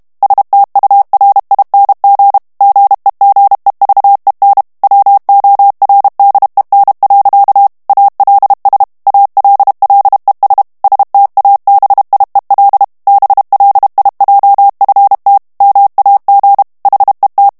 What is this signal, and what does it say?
Morse code, heard as a keyed tone, about 32 wpm, 800 Hz. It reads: STURING GEGEVEN WORDEN. ALS ALLES STABIEL BLIJFT MAG HET
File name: HOLLANDAIS texte clair 2 vitesse 30 mots minute